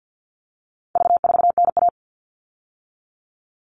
Morse code, heard as a keyed tone, about 50 wpm, 700 Hz. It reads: V4NA
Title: All Cut Numbers - 4-Digit 50wpm